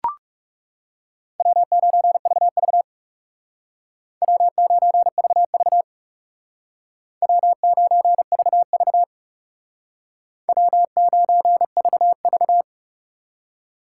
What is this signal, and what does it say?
Morse code, heard as a keyed tone, about 36 wpm, 700 Hz. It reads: W9VV W9VV W9VV W9VV
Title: Speed Racing - USA Call Signs 30wpm